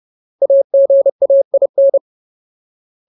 Morse code, heard as a keyed tone, about 30 wpm, 550 Hz. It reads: AGAIN
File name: Ultimate Word List / 30WPM Ultimate